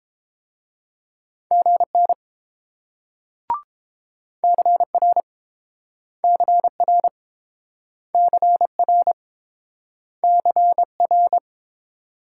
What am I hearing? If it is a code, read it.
Morse code, heard as a keyed tone, about 29 wpm, 700 Hz. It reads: GN CR CR CR CR